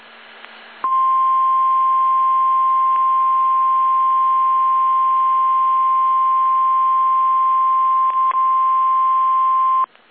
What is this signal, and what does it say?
Emergency Alert System (SAME) tones (the 1050 Hz attention tone). It is heard from 0.8 to 9.9 s.
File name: eas / KSDZKDJL / KSDZKDJL